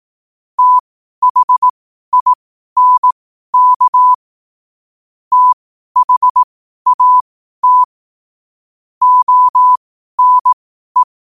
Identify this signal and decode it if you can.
Morse code, heard as a keyed tone, about 18 wpm, spaced as for 12 wpm, 1000 Hz. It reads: THINK THAT ONE
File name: qotdfiles / 12wpm